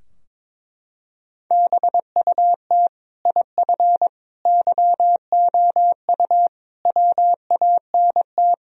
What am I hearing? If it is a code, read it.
Morse code, heard as a keyed tone, about 22 wpm, 700 Hz. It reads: BUT IF YOU WANT
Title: Sentences from Top 1000 Words - Review Part 3 - 22wpm